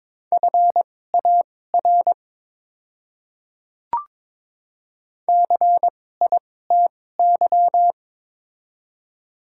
Morse code, heard as a keyed tone, about 22 wpm, spaced as for 16 wpm, 700 Hz. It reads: FAR CITY
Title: 2x Character Spacing - Top 300 Words - 22wpm